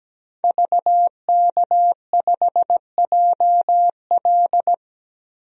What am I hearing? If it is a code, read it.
Morse code, heard as a keyed tone, about 17 wpm, 700 Hz. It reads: VK5JL
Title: DXCC-100 Call Signs - 5 Characters - 17wpm